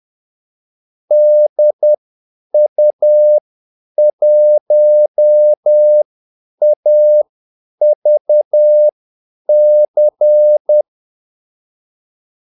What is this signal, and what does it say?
Morse code, heard as a keyed tone, about 10 wpm, spaced as for 8 wpm, 600 Hz. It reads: DU1AVC
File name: Callsigns-10-8wpm